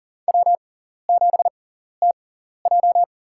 Morse code, heard as a keyed tone, about 40 wpm, spaced as for 14 wpm, 700 Hz. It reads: W7TJ
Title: USA Call Signs - 4 Characters - 6x Character Spacing - 40wpm